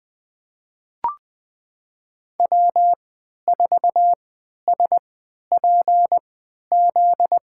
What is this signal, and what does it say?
Morse code, heard as a keyed tone, about 20 wpm, spaced as for 11 wpm, 700 Hz. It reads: W4SPZ
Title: USA Call Signs - 3x Character Spacing - 20wpm